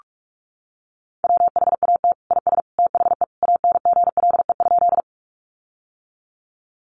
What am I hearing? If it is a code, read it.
Morse code, heard as a keyed tone, about 46 wpm, 700 Hz. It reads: WHAT IS THE ANGLE?
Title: Sentences from Top 1500 Words 45wpm